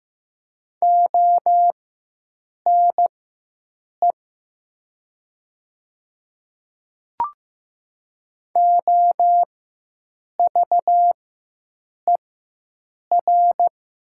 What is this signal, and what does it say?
Morse code, heard as a keyed tone, about 15 wpm, spaced as for 7 wpm, 700 Hz. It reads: ONE OVER
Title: 4x Character Spacing - Top 100 Words - 15wpm